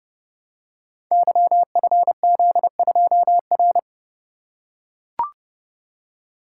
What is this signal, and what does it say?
Morse code, heard as a keyed tone, about 30 wpm, 700 Hz. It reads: YFZ2R
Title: Code Groups - 5 Letter:Numbers - 30wpm